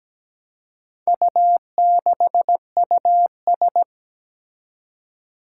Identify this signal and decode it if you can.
Morse code, heard as a keyed tone, about 17 wpm, 700 Hz. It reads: U6US
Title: Four Challenging Letter-Number Combinations 17wpm